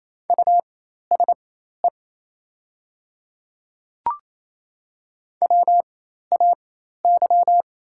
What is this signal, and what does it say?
Morse code, heard as a keyed tone, about 28 wpm, spaced as for 13 wpm, 690 Hz. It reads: USE WAY
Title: 4x Character Spacing - Top 200 Words - 28wpm